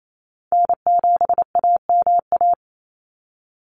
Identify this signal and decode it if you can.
Morse code, heard as a keyed tone, about 28 wpm, 700 Hz. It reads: N7AMA
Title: NA SOTA Chaser Callsigns 28wpm